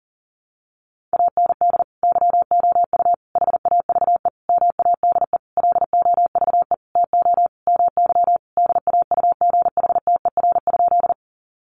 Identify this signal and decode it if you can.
Morse code, heard as a keyed tone, about 40 wpm, 700 Hz. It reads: AND YOU HAVE MADE LOVE TO MY DAUGHTER?
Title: Sentences from Top 1500 Words 40wpm